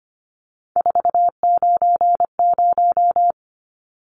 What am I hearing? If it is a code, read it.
Morse code, heard as a keyed tone, about 25 wpm, 700 Hz. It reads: 490
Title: Numbers - 3-Digit 25wpm